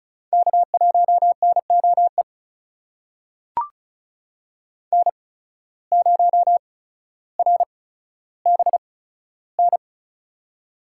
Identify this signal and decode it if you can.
Morse code, heard as a keyed tone, about 36 wpm, spaced as for 15 wpm, 700 Hz. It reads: K1NOE N0RBN
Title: USA Call Signs - 5 Characters - 8x Character Spacing - 35wpm